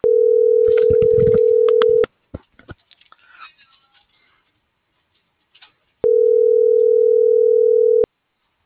Ringback tone (North America), 440 + 480 Hz.
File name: CENTRAL GENERAL / out rubyalejandra